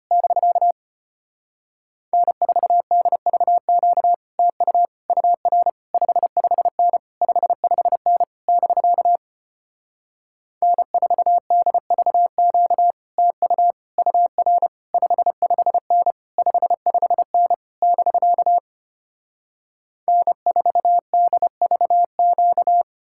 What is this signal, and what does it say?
Morse code, heard as a keyed tone, about 33 wpm, 700 Hz. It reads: <BK> N4DVQ TU UR 55N 55N <BK> N4DVQ TU UR 55N 55N <BK> N4DVQ